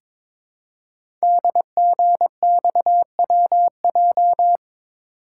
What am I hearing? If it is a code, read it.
Morse code, heard as a keyed tone, about 22 wpm, 700 Hz. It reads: DGXWJ
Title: Code Groups - 5 Letters - 22wpm